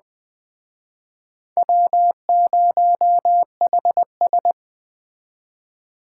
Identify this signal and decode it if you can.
Morse code, heard as a keyed tone, about 22 wpm, 700 Hz. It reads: W0HS